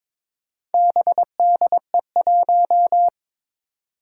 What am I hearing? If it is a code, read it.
Morse code, heard as a keyed tone, about 21 wpm, 700 Hz. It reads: BDE1
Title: Four Challenging Letter-Number Combinations 22wpm